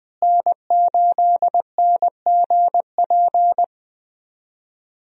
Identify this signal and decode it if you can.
Morse code, heard as a keyed tone, about 21 wpm, 700 Hz. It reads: N8NGP
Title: USA Call Signs - 5 Characters - Repeated 2x - 20wpm